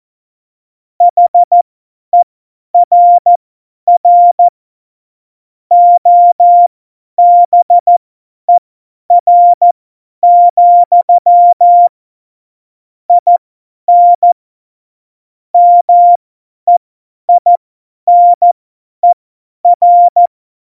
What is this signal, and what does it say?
Morse code, heard as a keyed tone, about 14 wpm, spaced as for 10 wpm, 700 Hz. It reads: HERR OBER, IN MEINER